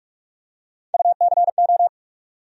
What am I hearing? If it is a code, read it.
Morse code, heard as a keyed tone, about 43 wpm, 700 Hz. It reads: UCO
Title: USA Call Sign Postfixes - 3 Letters 45wpm